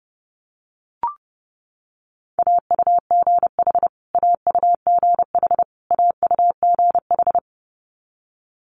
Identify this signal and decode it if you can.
Morse code, heard as a keyed tone, about 30 wpm, 700 Hz. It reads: AUGH AUGH AUGH
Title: Multi-letter Phonemes Repeated 3x 30wpm